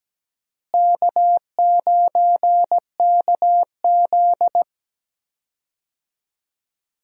Morse code, heard as a keyed tone, about 17 wpm, 700 Hz. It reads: K9KZ
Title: USA Call Signs - 4 Characters - Repeated 3x - 17wpm